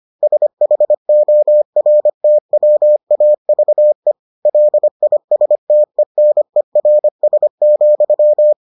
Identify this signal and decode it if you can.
Morse code, heard as a keyed tone, about 25 wpm, 600 Hz. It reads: SHORTWAVE LISTENERS,